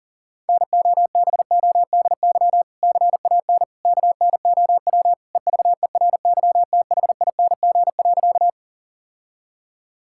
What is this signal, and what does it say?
Morse code, heard as a keyed tone, about 40 wpm, 700 Hz. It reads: NOBODY CAN KNOW EVERYTHING.
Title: Sentences from Top 1600 Words 40wpm